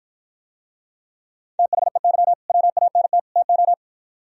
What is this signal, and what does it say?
Morse code, heard as a keyed tone, about 57 wpm, 700 Hz. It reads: THEY WANT TO